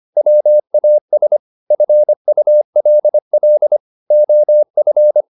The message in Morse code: WAS FULL OF

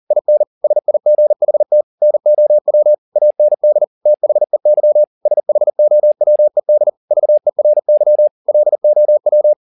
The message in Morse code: IN SIGHT NOW AND THEY SHOWED VERY LOW

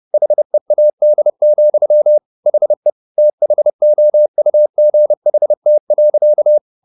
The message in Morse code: HEAD, HE THOUGHT.

